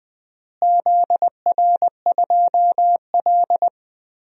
Morse code, heard as a keyed tone, about 20 wpm, 700 Hz. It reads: ZR2L
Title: DXCC-100 Call Signs - 4 Characters - Repeated 3x - 20wpm